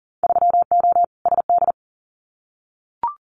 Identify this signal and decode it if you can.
Morse code, heard as a keyed tone, about 41 wpm, 700 Hz. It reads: 3O SD
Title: Winter Field Day Exchanges 40wpm